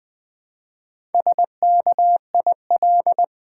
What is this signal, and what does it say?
Morse code, heard as a keyed tone, about 20 wpm, 700 Hz. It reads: SKIL
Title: Build A Word - 1 Letter At A Time - 5-Letter Words - 20wpm